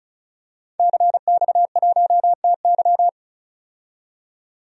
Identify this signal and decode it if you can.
Morse code, heard as a keyed tone, about 36 wpm, 700 Hz. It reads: CX1TY
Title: DXCC-100 Call Signs - 5 Characters - 35wpm